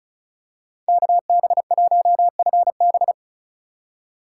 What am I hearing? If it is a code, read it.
Morse code, heard as a keyed tone, about 36 wpm, 700 Hz. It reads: KB1FB
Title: USA Call Signs - 5 Characters - 3x Character Spacing - 35wpm